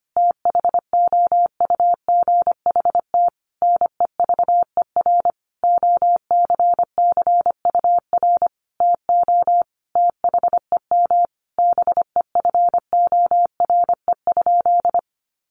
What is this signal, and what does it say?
Morse code, heard as a keyed tone, about 25 wpm, 700 Hz. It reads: THOUGHT NEVER OCCUR TO THEM BEFORE?
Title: Sentences from Top 900 Words 25wpm